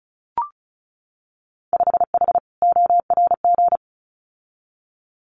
Morse code, heard as a keyed tone, about 36 wpm, 700 Hz. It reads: 5H ORG
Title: Winter Field Day Exchanges Encore 35wpm